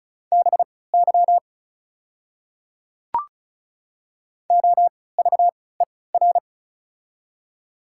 Morse code, heard as a keyed tone, about 36 wpm, spaced as for 20 wpm, 700 Hz. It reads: BY OVER
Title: 3x Character Spacing - Top 200 Words - 35wpm